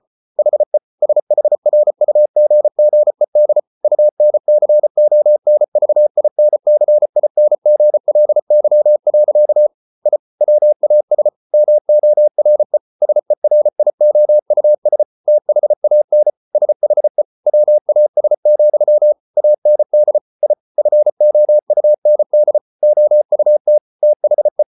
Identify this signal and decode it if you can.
Morse code, heard as a keyed tone, about 34 wpm, 600 Hz. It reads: HE SHRUGGED UNCONVINCINGLY. I WAS MORE SERIOUS THAN SHE WAS, AND I FOUND OUT THE